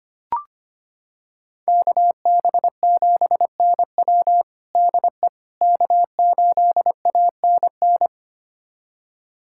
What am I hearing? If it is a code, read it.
Morse code, heard as a keyed tone, about 25 wpm, 700 Hz. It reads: KB7NW DE K8ANN
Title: USA Callsign DE Callsign - 5 Characters - 25wpm